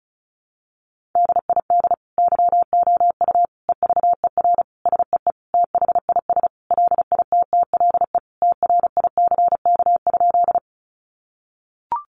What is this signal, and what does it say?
Morse code, heard as a keyed tone, about 34 wpm, 700 Hz. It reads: DID YOU EVER SEE THIS LITTLE TRICK?
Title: Sentences from Top 1500 Words 35wpm